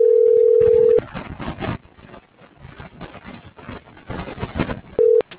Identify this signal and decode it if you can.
Ringback tone (North America), 440 + 480 Hz.